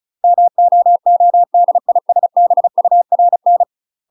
Morse code, heard as a keyed tone, about 36 wpm, 700 Hz. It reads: MOODISBURN
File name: UK MASTER QSOs TRAINING / 35WPM / UK MASTER QSOs 0010